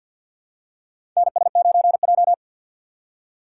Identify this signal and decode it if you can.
Morse code, heard as a keyed tone, about 48 wpm, 700 Hz. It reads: NS9J
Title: USA Call Signs - 4 Characters - 6x Character Spacing - 50wpm